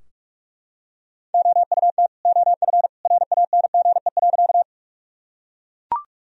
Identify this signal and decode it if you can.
Morse code, heard as a keyed tone, about 41 wpm, 700 Hz. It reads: OUT OF RANGE.
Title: Sentences from Top 1000 Words - Review Part 4 - 45wpm